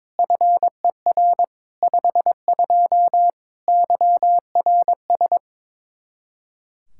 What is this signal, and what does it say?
Morse code, heard as a keyed tone, about 22 wpm, 700 Hz. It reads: FER 52 YRS
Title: Practice QSO - Exchange 3 - More Info - 22wpm